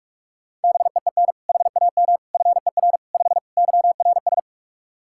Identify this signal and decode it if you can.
Morse code, heard as a keyed tone, about 46 wpm, 700 Hz. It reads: BEEN HAM FER 5 YRS
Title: Practice QSO - Exchange 3 - More Info - 45wpm